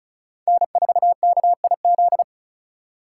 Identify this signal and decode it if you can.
Morse code, heard as a keyed tone, about 43 wpm, 700 Hz. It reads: N4KIZ
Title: Speed Racing - USA 1x3 Call Signs 35wpm